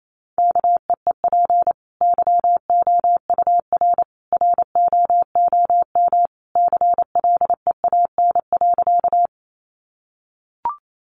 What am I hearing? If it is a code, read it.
Morse code, heard as a keyed tone, about 28 wpm, 700 Hz. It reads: KEEP YOUR ROOM CLEAN.
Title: Sentences from Top 1100 Words 28wpm